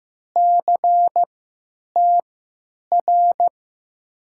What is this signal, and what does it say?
Morse code, heard as a keyed tone, about 15 wpm, spaced as for 9 wpm, 700 Hz. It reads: CTR
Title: USA Call Signs - 6 Characters - 3x Character Spacing - 15wpm